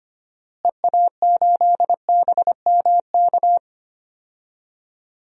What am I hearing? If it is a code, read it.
Morse code, heard as a keyed tone, about 25 wpm, 700 Hz. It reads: EA8BMK
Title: DXCC-100 Call Signs - 6 Characters - Repeated 2x - 25wpm